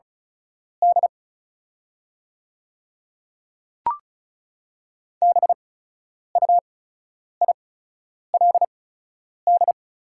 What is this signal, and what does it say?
Morse code, heard as a keyed tone, about 36 wpm, spaced as for 10 wpm, 690 Hz. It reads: D BUILD
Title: 8x Character Spacing - Top 500 Words - 35wpm